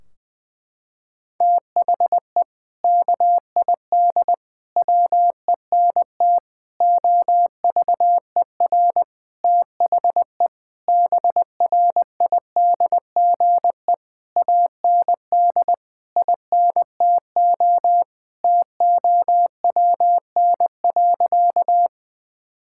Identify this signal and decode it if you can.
Morse code, heard as a keyed tone, about 20 wpm, 700 Hz. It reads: THE KID WENT OVER THE BRIDGE AND INTO TOWN.
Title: Sentences from Top 1600 Words 20wpm